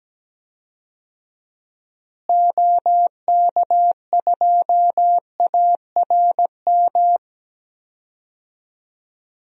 Morse code, heard as a keyed tone, about 17 wpm, 700 Hz. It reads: OK2ARM
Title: DXCC-100 Call Signs - 6 Characters - Repeated 3x - 17wpm